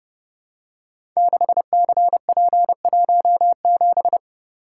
Morse code, heard as a keyed tone, about 30 wpm, 700 Hz. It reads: 6CP17
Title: Code Groups - 5 Letter:Numbers - 30wpm